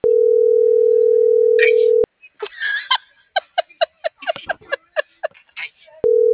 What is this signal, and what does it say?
Ringback tone (North America), 440 + 480 Hz.
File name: CENTRAL MAYAB / out vianey.prado - .72169